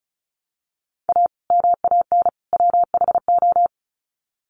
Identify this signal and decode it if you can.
Morse code, heard as a keyed tone, about 34 wpm, 700 Hz. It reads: A MAN WHO
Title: Sets of 3 Words 35wpm